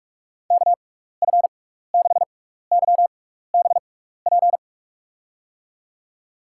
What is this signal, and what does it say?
Morse code, heard as a keyed tone, about 43 wpm, spaced as for 15 wpm, 700 Hz. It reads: KF6YBP
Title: USA Call Signs - 6 Characters - 6x Character Spacing - 45wpm